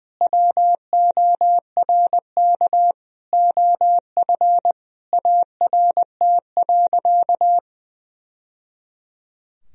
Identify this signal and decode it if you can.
Morse code, heard as a keyed tone, about 20 wpm, 700 Hz. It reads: WORK OF ART.
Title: Sentences from Top 1000 Words - Review Part 3 - 20wpm